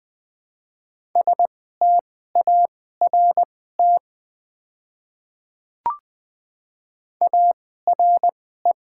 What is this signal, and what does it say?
Morse code, heard as a keyed tone, about 20 wpm, spaced as for 14 wpm, 700 Hz. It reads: START ARE
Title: 2x Character Spacing - Top 300 Words - 20wpm